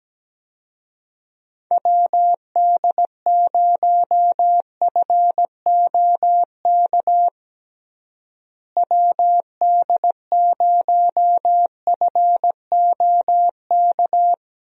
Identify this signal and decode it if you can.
Morse code, heard as a keyed tone, about 17 wpm, 700 Hz. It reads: WD0FOK WD0FOK
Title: USA Call Signs - 6 Characters - Repeated 3x - 17wpm